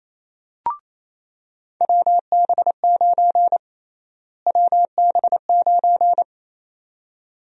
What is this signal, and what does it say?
Morse code, heard as a keyed tone, about 28 wpm, 700 Hz. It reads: WB9 WB9
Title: USA Call Sign Prefixes - Repeated 2x - 28wpm